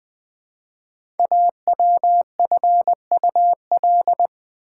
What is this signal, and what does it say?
Morse code, heard as a keyed tone, about 20 wpm, 700 Hz. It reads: AWFUL
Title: 5 Letter Words - First 3 Letters Given - 20wpm